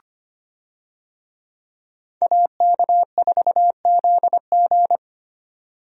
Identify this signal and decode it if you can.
Morse code, heard as a keyed tone, about 25 wpm, 700 Hz. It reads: AK4ZG
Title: USA Call Signs - 5 Characters - Repeated 2x - 25wpm